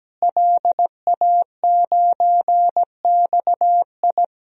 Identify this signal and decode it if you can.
Morse code, heard as a keyed tone, about 17 wpm, 700 Hz. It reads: LA9XI
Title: DXCC-100 Call Signs - 5 Characters - Repeated 3x - 17wpm